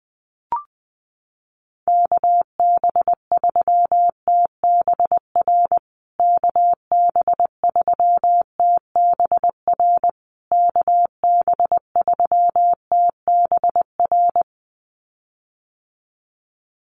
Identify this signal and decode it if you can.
Morse code, heard as a keyed tone, about 20 wpm, 700 Hz. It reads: KB3TBR KB3TBR KB3TBR
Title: USA Call Signs Repeated 3x 20wpm